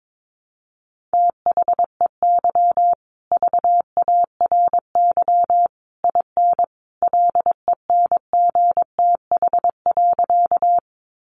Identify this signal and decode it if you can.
Morse code, heard as a keyed tone, about 22 wpm, 700 Hz. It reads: THEY VARY IN LENGTH.
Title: Sentences from Top 1200 Words 22wpm